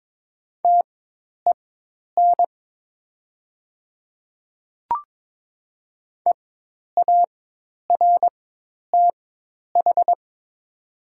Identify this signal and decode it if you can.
Morse code, heard as a keyed tone, about 22 wpm, spaced as for 10 wpm, 700 Hz. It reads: TEN EARTH
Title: 4x Character Spacing - Top 500 Words - 22wpm